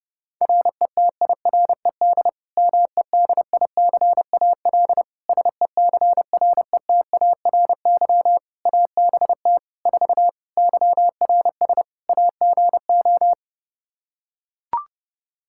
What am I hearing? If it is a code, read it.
Morse code, heard as a keyed tone, about 31 wpm, 700 Hz. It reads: RETIRED MEDICAL SECRETARY ABT 4 YRS AGO
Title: Practice QSO - Exchange 3 - More Info - 30wpm